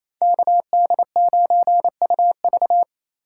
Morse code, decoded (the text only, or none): KD9UV